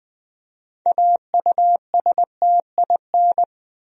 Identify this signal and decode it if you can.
Morse code, heard as a keyed tone, about 20 wpm, 700 Hz. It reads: AUSTIN
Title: USA Common Names - 20wpm